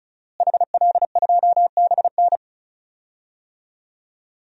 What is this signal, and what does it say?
Morse code, heard as a keyed tone, about 36 wpm, 700 Hz. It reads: HL2BN
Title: DXCC-100 Call Signs - 5 Characters - Repeated 2x - 35wpm